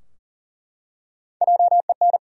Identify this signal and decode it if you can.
Morse code, heard as a keyed tone, about 40 wpm, 700 Hz. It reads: JEN
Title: USA Common Names - 40wpm